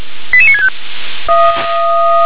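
Five-tone selective call (ZVEI), heard as quick runs of five tones, tones of about 70 ms. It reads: 88965
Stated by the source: ZVEI five-tone selective call received by SDR.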